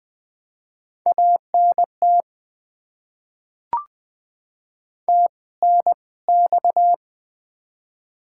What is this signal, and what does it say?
Morse code, heard as a keyed tone, about 20 wpm, spaced as for 17 wpm, 690 Hz. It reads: ANT TNX
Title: Top 20 QSO Elements - 2x Character Spacing - 20wpm